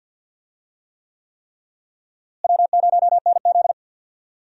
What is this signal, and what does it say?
Morse code, heard as a keyed tone, about 52 wpm, 700 Hz. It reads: W0NZ
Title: USA Call Signs - 4 Characters - Repeated 3x - 50wpm